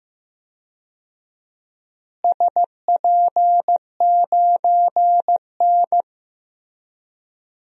Morse code, heard as a keyed tone, about 15 wpm, 700 Hz. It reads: SP9N
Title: DXCC-100 Call Signs - 4 Characters - Repeated 2x - 15wpm